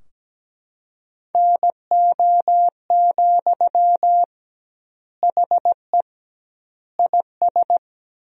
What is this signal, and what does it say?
Morse code, heard as a keyed tone, about 17 wpm, 700 Hz. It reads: NO, HE IS
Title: 2x Word-Spacing - Sentences from Top 400 Words - 17wpm